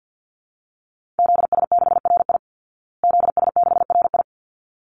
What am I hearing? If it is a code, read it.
Morse code, heard as a keyed tone, about 50 wpm, 700 Hz. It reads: ZS6RI ZS6RI
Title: Call Signs Repeated 3x - 50wpm